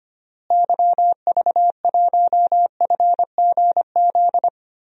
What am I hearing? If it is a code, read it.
Morse code, heard as a keyed tone, about 25 wpm, 700 Hz. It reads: YV1FGZ
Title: DXCC-100 Call Signs - 6 Characters - Repeated 3x - 25wpm